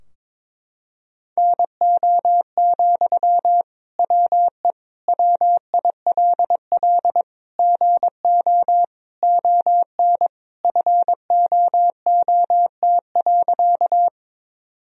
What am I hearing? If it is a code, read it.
Morse code, heard as a keyed tone, about 22 wpm, 700 Hz. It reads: NO, WE WILL GO ON FOOT.
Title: Sentences from Top 1000 Words - Review Part 3 - 22wpm